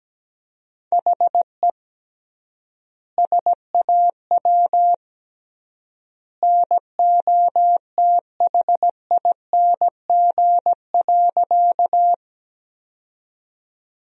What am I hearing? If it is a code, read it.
Morse code, heard as a keyed tone, about 17 wpm, 700 Hz. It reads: HE SAW NOTHING.
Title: 3x Word-Spacing - Sentences from Top 500 Words - 17wpm